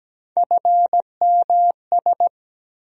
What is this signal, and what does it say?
Morse code, heard as a keyed tone, about 17 wpm, 700 Hz. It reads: FMS
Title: USA Call Sign Postfixes - 3 Letters 17wpm